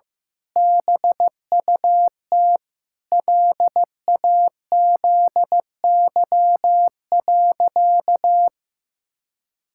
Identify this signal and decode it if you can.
Morse code, heard as a keyed tone, about 15 wpm, 700 Hz. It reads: BUT LAZY.
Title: Pangram Challenge v2 - 15wpm